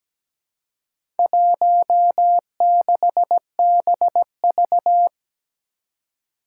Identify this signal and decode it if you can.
Morse code, heard as a keyed tone, about 17 wpm, 700 Hz. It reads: 16BV
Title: Four Challenging Letter-Number Combinations 17wpm